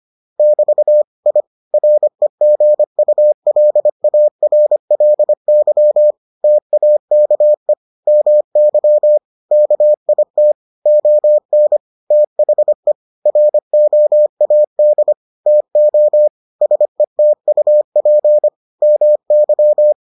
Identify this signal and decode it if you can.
Morse code, heard as a keyed tone, about 25 wpm, 600 Hz. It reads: <BT> I REGULARLY TAKE MY KIT ON THE ROAD TO SETUP MY